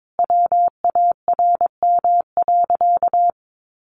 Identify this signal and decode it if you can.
Morse code, heard as a keyed tone, about 22 wpm, 700 Hz. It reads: WARM.
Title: Sentences from Top 1500 Words 22wpm